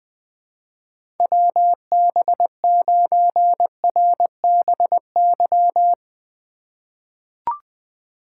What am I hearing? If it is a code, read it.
Morse code, heard as a keyed tone, about 20 wpm, 700 Hz. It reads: WB9RBY
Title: USA Call Signs - 2x Character Spacing - 20wpm